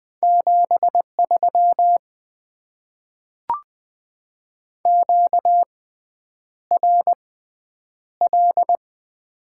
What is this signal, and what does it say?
Morse code, heard as a keyed tone, about 20 wpm, spaced as for 9 wpm, 700 Hz. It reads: 73 QRL
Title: Top 20 QSO Elements - 6x Character Spacing - 20wpm